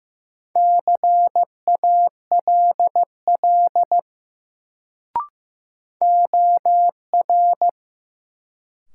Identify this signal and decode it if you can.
Morse code, heard as a keyed tone, about 15 wpm, 700 Hz. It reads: CALL OR
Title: Sentences from Top 100 Words - New Word Drill - 15wpm